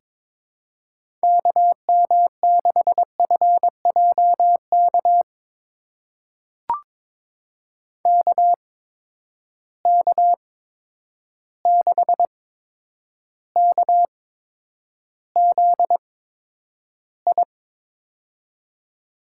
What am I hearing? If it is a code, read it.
Morse code, heard as a keyed tone, about 22 wpm, spaced as for 9 wpm, 700 Hz. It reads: KM6FJK KK6KZI
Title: USA Call Signs - 6 Characters - 8x Character Spacing - 22wpm